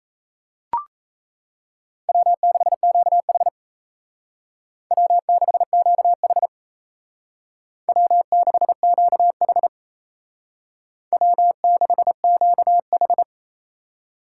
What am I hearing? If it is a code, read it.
Morse code, heard as a keyed tone, about 35 wpm, 700 Hz. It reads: W6QH W6QH W6QH W6QH
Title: Speed Racing - USA 1x2 Call Signs 28wpm